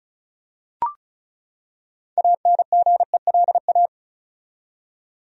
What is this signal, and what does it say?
Morse code, heard as a keyed tone, about 36 wpm, 700 Hz. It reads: ANGELA
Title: USA Common Names - 35wpm